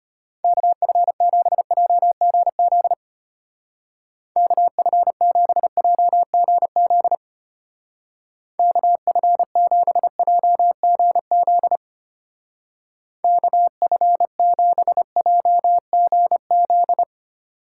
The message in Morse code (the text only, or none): KF7JGZ KF7JGZ KF7JGZ KF7JGZ